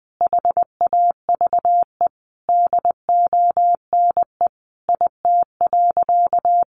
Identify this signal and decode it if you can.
Morse code, heard as a keyed tone, about 20 wpm, 700 Hz. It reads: HAVE DONE IT.
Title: Sentences from Top 1100 Words 20wpm